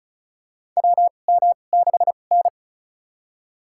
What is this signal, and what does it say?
Morse code, heard as a keyed tone, about 36 wpm, spaced as for 25 wpm, 700 Hz. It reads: WM6N
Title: USA Call Signs - 4 Characters - 2x Character Spacing - 35wpm